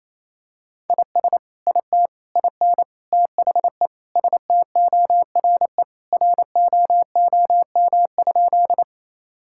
Morse code, heard as a keyed tone, about 28 wpm, 700 Hz. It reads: IS IT IN THE STORE ROOM?
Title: Sentences from Top 1000 Words - Review Part 5 - 28wpm